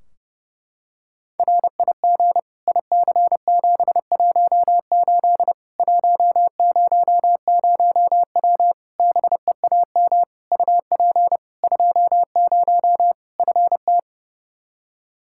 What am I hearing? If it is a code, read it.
Morse code, heard as a keyed tone, about 30 wpm, 700 Hz. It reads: RIG IC718 100W BEAM UP 20 FT
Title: K1USN SST QSO Practice 30wpm